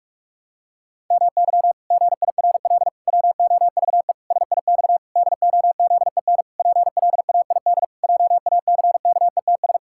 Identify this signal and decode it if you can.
Morse code, heard as a keyed tone, about 46 wpm, 700 Hz. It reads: MY GIRL WOVE SIX DOZEN PLAID JACKETS
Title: Pangram Challenge v2 - 45wpm